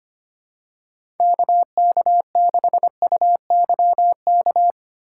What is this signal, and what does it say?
Morse code, heard as a keyed tone, about 25 wpm, 700 Hz. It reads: KK6UYK
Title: USA Call Signs - 6 Characters - 4x Character Spacing - 25wpm